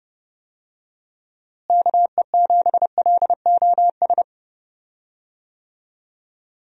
Morse code, heard as a keyed tone, about 30 wpm, 700 Hz. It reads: KE7LOS
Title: USA Call Signs - 6 Characters - Repeated 3x - 30wpm